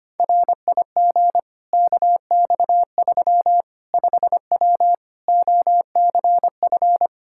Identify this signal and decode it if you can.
Morse code, heard as a keyed tone, about 25 wpm, 700 Hz. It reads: RIG KX3 5W OCF
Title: Practice QSO - Exchange 2 - Info - 25wpm